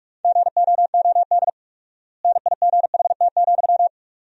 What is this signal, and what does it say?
Morse code, heard as a keyed tone, about 42 wpm, 700 Hz. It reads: GOOD NIGHT,
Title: 4x Word-Spacing - Sentences from Top 300 Words - 45wpm